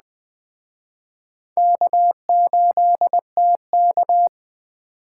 Morse code, heard as a keyed tone, about 21 wpm, 700 Hz. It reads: K8TK